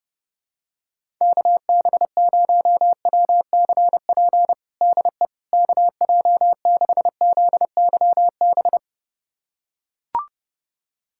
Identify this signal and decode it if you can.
Morse code, heard as a keyed tone, about 32 wpm, 700 Hz. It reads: KB0WCP DE KJ6ZYB
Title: USA Callsign DE Callsign - 6 Characters - 30wpm